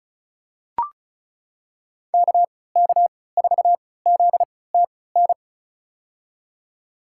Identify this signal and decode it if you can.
Morse code, heard as a keyed tone, about 36 wpm, spaced as for 20 wpm, 700 Hz. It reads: KK4ZTN
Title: USA Call Signs - 6 Characters - 3x Character Spacing - 35wpm